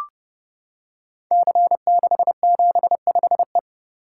Morse code, heard as a keyed tone, about 30 wpm, 700 Hz. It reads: C675E
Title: Code Groups - 5 Letter:Numbers - 30wpm